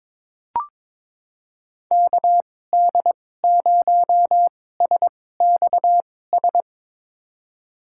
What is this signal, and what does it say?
Morse code, heard as a keyed tone, about 22 wpm, spaced as for 16 wpm, 700 Hz. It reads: KD0SXS
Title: USA Call Signs - 6 Characters - 2x Character Spacing - 22wpm